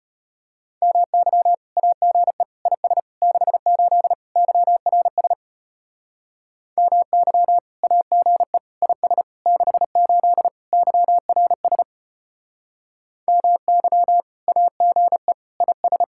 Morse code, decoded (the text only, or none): MY AGE IS 68 YRS MY AGE IS 68 YRS MY AGE IS